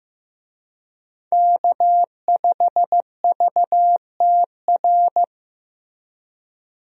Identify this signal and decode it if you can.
Morse code, heard as a keyed tone, about 14 wpm, 700 Hz. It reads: K5VTR
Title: USA Call Signs - 5 Characters - Repeated 2x - 15wpm